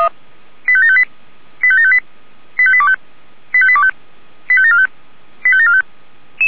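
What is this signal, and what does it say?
Five-tone selective call (ZVEI), heard as quick runs of five tones, tones of about 70 ms. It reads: 85758 85758 85825 85825 85745 85745
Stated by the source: ZVEI five-tone call 85726 received on SDR.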